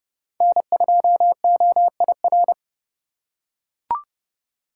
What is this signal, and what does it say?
Morse code, heard as a keyed tone, about 30 wpm, 700 Hz. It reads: N2OIR
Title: USA Call Signs - 5 Characters - 2x Character Spacing - 30wpm